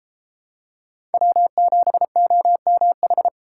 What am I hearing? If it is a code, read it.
Morse code, heard as a keyed tone, about 29 wpm, 700 Hz. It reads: W7OMH